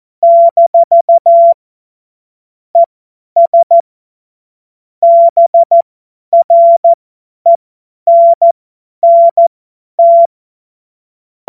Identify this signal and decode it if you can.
Morse code, heard as a keyed tone, about 14 wpm, spaced as for 10 wpm, 680 Hz. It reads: - ES BRENNT